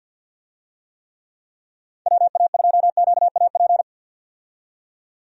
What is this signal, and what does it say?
Morse code, heard as a keyed tone, about 50 wpm, 700 Hz. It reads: WA2QAP
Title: USA Call Signs - 6 Characters - 50wpm